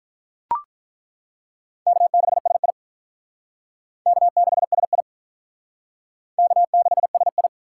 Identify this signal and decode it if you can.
Morse code, heard as a keyed tone, about 42 wpm, 700 Hz. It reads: K6SI K6SI K6SI
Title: Speed Racing - USA 1x2 Call Signs 35wpm